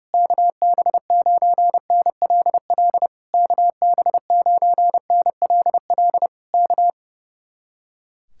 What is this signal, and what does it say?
Morse code, heard as a keyed tone, about 29 wpm, 700 Hz. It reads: KB9NLL KB9NLL K
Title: Speed-Racing Common QSO Phrases - 30wpm